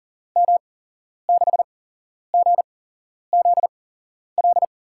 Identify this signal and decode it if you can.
Morse code, heard as a keyed tone, about 41 wpm, spaced as for 11 wpm, 700 Hz. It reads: M6GZL
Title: USA Call Signs - 6 Characters - 8x Character Spacing - 40wpm